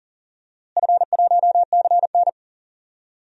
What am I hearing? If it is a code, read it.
Morse code, heard as a keyed tone, about 39 wpm, 700 Hz. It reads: F1CN
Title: DXCC-100 Call Signs - 4 Characters - Repeated 3x - 40wpm